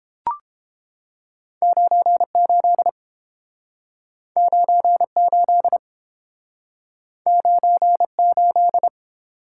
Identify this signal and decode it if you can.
Morse code, heard as a keyed tone, about 30 wpm, 700 Hz. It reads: 98 98 98